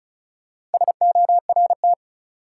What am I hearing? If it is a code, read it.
Morse code, heard as a keyed tone, about 34 wpm, 700 Hz. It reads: SORT
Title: Sets of 2 Words - New Word Drill - 35wpm